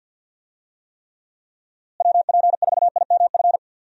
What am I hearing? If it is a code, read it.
Morse code, heard as a keyed tone, about 48 wpm, 700 Hz. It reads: WP4IMF
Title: USA Call Signs - 6 Characters - Repeated 2x - 50wpm